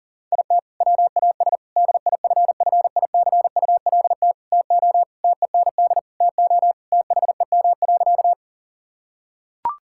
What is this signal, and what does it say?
Morse code, heard as a keyed tone, about 40 wpm, 700 Hz. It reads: IT WAS DIFFICULT TO TEND TO THEM.
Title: Sentences from Top 1000 Words - Review Part 5 - 40wpm